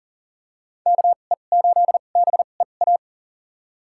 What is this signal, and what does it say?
Morse code, heard as a keyed tone, about 40 wpm, spaced as for 29 wpm, 700 Hz. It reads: KE8BEA
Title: USA Call Signs - 2x Character Spacing - 40wpm